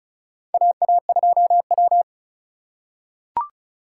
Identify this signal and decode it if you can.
Morse code, heard as a keyed tone, about 36 wpm, 700 Hz. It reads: AA2W
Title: USA Call Signs - 4x Character Spacing - 35wpm